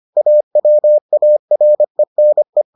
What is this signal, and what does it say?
Morse code, heard as a keyed tone, about 25 wpm, 600 Hz. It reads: AWARENE